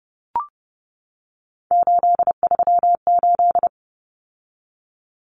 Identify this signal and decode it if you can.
Morse code, heard as a keyed tone, about 30 wpm, 700 Hz. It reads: 838
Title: Numbers - 3-Digit 30wpm